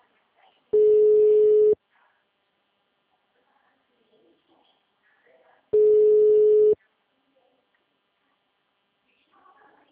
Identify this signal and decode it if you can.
Ringback tone (Europe, 425 Hz), with speech.
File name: out- -2028- - - .145